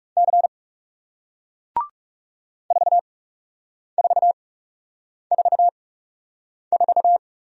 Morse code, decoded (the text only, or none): C 4444